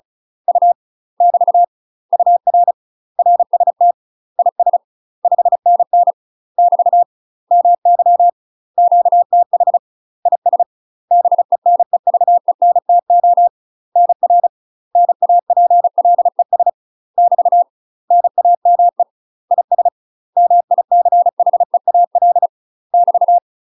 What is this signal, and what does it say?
Morse code, heard as a keyed tone, about 36 wpm, 700 Hz. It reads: U <BT> UR RST IS 5NN <BT> MY QTH IS BENEVENTO NR NAPLES <BT> NAME IS MICHEAL <BT>
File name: UK MASTER QSOs TRAINING / 35WPM / UK MASTER QSOs 0036